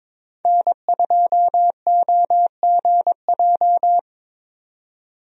Morse code, heard as a keyed tone, about 22 wpm, 700 Hz. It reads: N2OGJ
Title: USA Call Signs - 5 Characters - 22wpm